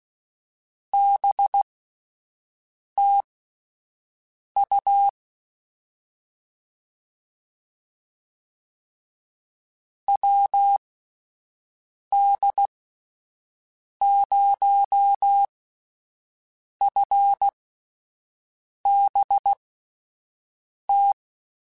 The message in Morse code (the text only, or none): BT U WD0FBT